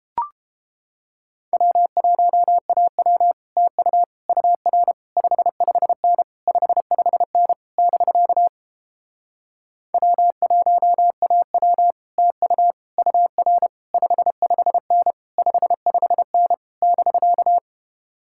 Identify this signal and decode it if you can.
Morse code, heard as a keyed tone, about 31 wpm, 700 Hz. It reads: W1AW TU UR 55N 55N <BK> W1AW TU UR 55N 55N <BK>